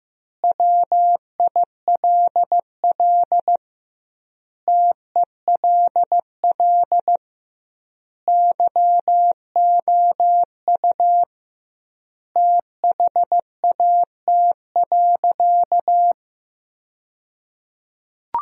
Morse code, heard as a keyed tone, about 15 wpm, 700 Hz. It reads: WILL TELL YOU THAT.
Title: 2x Word-Spacing - Sentences from Top 300 Words - 15wpm